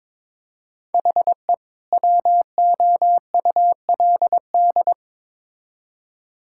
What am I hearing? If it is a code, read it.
Morse code, heard as a keyed tone, about 21 wpm, 700 Hz. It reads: HE WOULD
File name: Build A Phrase - 2-Word Phrases - 22wpm